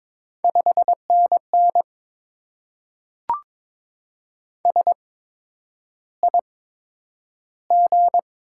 Morse code, heard as a keyed tone, about 22 wpm, spaced as for 9 wpm, 700 Hz. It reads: 5NN SIG
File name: Top 90 QSO Elements - 8x Character Spacing - 22wpm